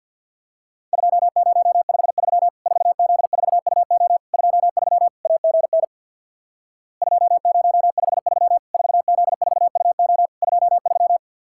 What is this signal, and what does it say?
Morse code, heard as a keyed tone, about 52 wpm, 700 Hz. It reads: 2053 4Z4UO 23 AGN 2053 4Z4UO 23
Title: Worked All Europe DX Contest - QTC Exchange with Repeats - 50wpm